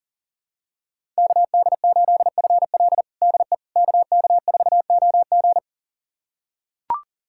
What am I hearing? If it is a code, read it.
Morse code, heard as a keyed tone, about 40 wpm, 700 Hz. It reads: KD8FL DE KK4OG
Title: USA Callsign DE Callsign - 5 Characters - 40wpm